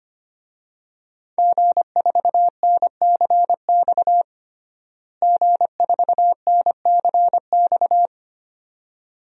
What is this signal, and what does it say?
Morse code, heard as a keyed tone, about 25 wpm, 700 Hz. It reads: G4NCX G4NCX
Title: DXCC-100 Call Signs - 5 Characters - Repeated 2x - 25wpm